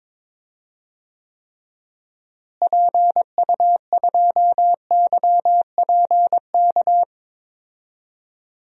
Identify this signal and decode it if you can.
Morse code, heard as a keyed tone, about 21 wpm, 700 Hz. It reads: PU2YPK
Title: DXCC-100 Call Signs - 6 Characters - Repeated 2x - 22wpm